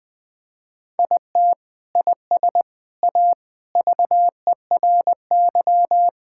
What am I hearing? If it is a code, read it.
Morse code, heard as a keyed tone, about 20 wpm, 700 Hz. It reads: IT IS A VERY
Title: Sentences from Top 1600 Words 20wpm